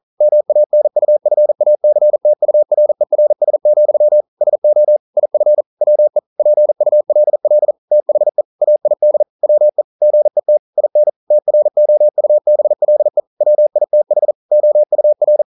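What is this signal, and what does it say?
Morse code, heard as a keyed tone, about 42 wpm, 600 Hz. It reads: MANUFACTURERS, SO IF WE PULL THE AID WE GET IN TROUBLE WITH OUR
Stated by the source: Morse code practice at 41 wpm, from CW med Gnister.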